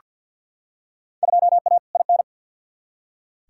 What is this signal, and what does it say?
Morse code, heard as a keyed tone, about 47 wpm, 700 Hz. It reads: 2A IN
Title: Field Day Exchanges - 50wpm